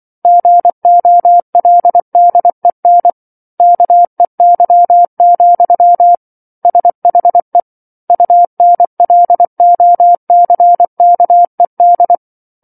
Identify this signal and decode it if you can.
Morse code, heard as a keyed tone, about 24 wpm, 700 Hz. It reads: GOLDEN KEY, SHE UNLOCKED
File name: 07-Chapter-07 24 WPM